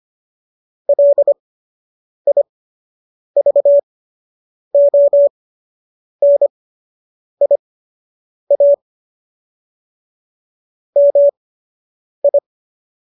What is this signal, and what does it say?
Morse code, heard as a keyed tone, about 25 wpm, spaced as for 8 wpm, 570 Hz. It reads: LIVONIA MI
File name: qso006 08